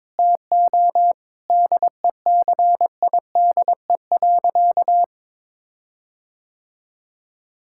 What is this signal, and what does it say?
Morse code, heard as a keyed tone, about 22 wpm, 700 Hz. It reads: TO DECIDE.
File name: practice / 22wpm / Sentences with 8-Word Phrases - 22wpm